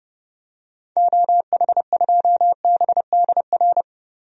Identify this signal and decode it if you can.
Morse code, heard as a keyed tone, about 32 wpm, 700 Hz. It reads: OH2BDR
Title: DXCC-100 Call Signs - 6 Characters - Repeated 2x - 30wpm